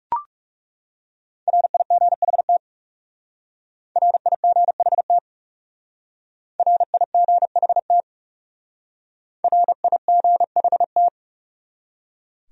Morse code, heard as a keyed tone, about 36 wpm, 700 Hz. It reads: RIGHT RIGHT RIGHT RIGHT
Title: Speed-Racing Top 300 Words 30wpm